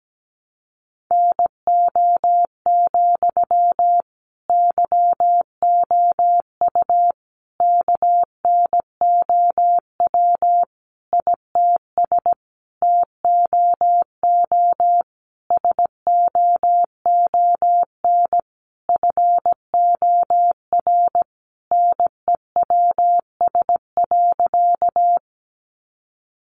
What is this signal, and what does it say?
Morse code, heard as a keyed tone, about 17 wpm, 700 Hz. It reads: NO, YOU KNOW ITS TOO SOON FOR NEWS.
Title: Sentences from Top 800 Words 17wpm